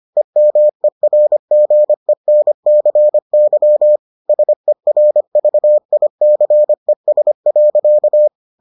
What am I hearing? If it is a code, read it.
Morse code, heard as a keyed tone, about 25 wpm, 600 Hz. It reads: EMERGENCY SERVICES.